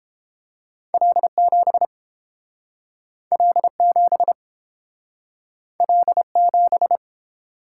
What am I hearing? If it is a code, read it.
Morse code, heard as a keyed tone, about 30 wpm, 700 Hz. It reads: L7 L7 L7